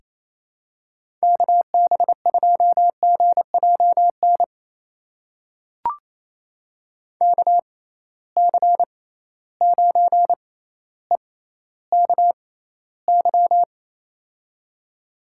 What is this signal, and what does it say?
Morse code, heard as a keyed tone, about 28 wpm, spaced as for 14 wpm, 700 Hz. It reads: KB2GJN KC9EKY
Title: USA Call Signs - 6 Characters - 6x Character Spacing - 28wpm